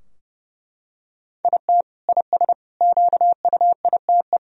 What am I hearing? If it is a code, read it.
Morse code, heard as a keyed tone, about 32 wpm, 700 Hz. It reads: IT IS QUITE
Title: Sentences from Top 1000 Words - Review Part 3 - 30wpm